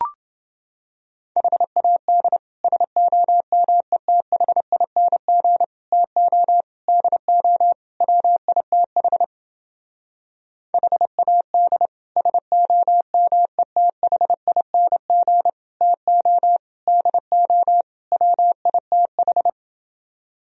Morse code, heard as a keyed tone, about 30 wpm, 700 Hz. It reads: HAD SOMETHING TO DO WITH HAD SOMETHING TO DO WITH